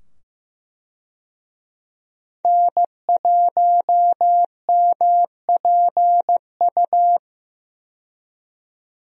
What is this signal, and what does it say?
Morse code, heard as a keyed tone, about 16 wpm, 700 Hz. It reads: N1MPU
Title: USA Call Signs - 5 Characters - 15wpm